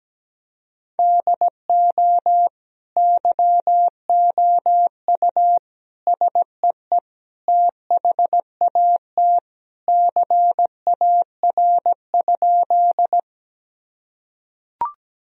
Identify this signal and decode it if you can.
Morse code, heard as a keyed tone, about 17 wpm, 700 Hz. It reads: DO YOU SEE THAT CAR?
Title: Sentences from Top 1000 Words - Review Part 1 - 17wpm